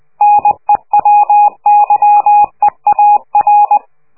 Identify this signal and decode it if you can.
Morse code, heard as a keyed tone, about 20 wpm, 830 Hz. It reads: NEW YEAR
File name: worldcoastal / scheveningenradio / morsepch